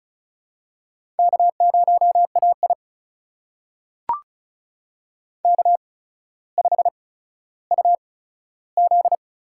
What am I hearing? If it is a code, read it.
Morse code, heard as a keyed tone, about 36 wpm, spaced as for 15 wpm, 700 Hz. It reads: K0AI K5UZ
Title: USA Call Signs - 4 Characters - 8x Character Spacing - 35wpm